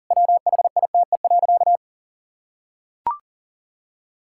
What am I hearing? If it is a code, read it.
Morse code, heard as a keyed tone, about 40 wpm, 700 Hz. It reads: WHITE.
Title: Sentences from Top 1600 Words 40wpm